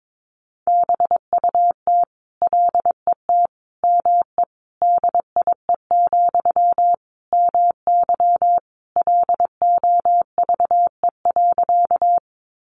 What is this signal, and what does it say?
Morse code, heard as a keyed tone, about 22 wpm, 700 Hz. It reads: BUT LET ME DIE, MY LOVE.
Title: Sentences from Top 1300 Words 22wpm